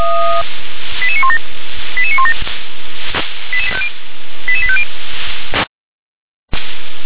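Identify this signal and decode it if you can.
Five-tone selective call (ZVEI), heard as quick runs of five tones, tones of about 70 ms. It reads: 88917 88917 88850
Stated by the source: ZVEI five-tone call received by SDR.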